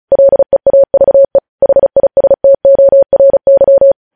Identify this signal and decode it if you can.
Morse code, heard as a keyed tone, about 36 wpm, 550 Hz. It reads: LEAVE HISTORY
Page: News headlines in Morse code at 35 wpm